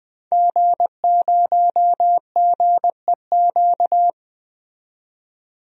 Morse code, heard as a keyed tone, about 21 wpm, 700 Hz. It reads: G0GEQ
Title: DXCC-100 Call Signs - 5 Characters - 20wpm